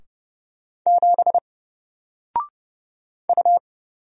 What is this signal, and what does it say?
Morse code, heard as a keyed tone, about 30 wpm, 700 Hz. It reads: 7 U
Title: Single Character (Letter:Number:Punctuation:Prosign) Alt 30wpm